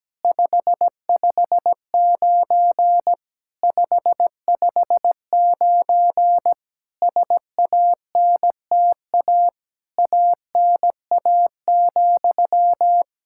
Morse code, heard as a keyed tone, about 17 wpm, 700 Hz. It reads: 559 559 SANTA ANA,
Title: Practice QSO - Exchange 1 - Beginning - 17wpm